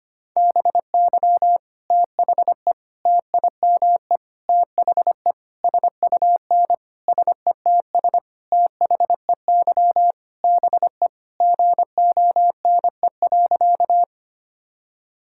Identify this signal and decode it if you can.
Morse code, heard as a keyed tone, about 25 wpm, 700 Hz. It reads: BY THE TIME THE SUN SETS THEY BE GONE.
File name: practice / 25wpm / Sentences with 4-Word Phrases - 25wpm